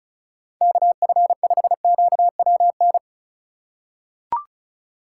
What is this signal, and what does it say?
Morse code, heard as a keyed tone, about 34 wpm, 700 Hz. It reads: KF5QWN
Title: USA Call Signs - 6 Characters - 3x Character Spacing - 35wpm